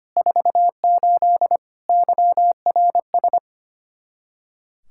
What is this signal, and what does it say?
Morse code, heard as a keyed tone, about 25 wpm, 700 Hz. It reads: 48 YRS
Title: K1USN SST QSO Practice 25wpm